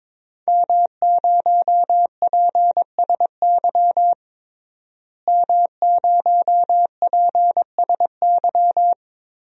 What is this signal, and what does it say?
Morse code, heard as a keyed tone, about 22 wpm, 700 Hz. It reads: M0PSY M0PSY
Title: DXCC-100 Call Signs - 5 Characters - Repeated 2x - 22wpm